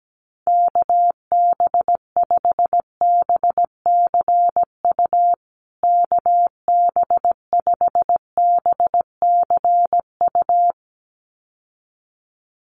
Morse code, heard as a keyed tone, about 17 wpm, 700 Hz. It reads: KB5BCU KB5BCU
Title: USA Call Signs Repeated 2x 17wpm